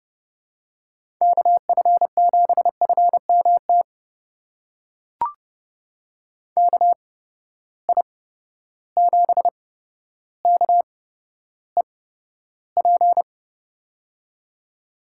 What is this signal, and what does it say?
Morse code, heard as a keyed tone, about 30 wpm, spaced as for 13 wpm, 700 Hz. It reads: KF7FMT KI7KEP
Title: USA Call Signs - 6 Characters - 8x Character Spacing - 30wpm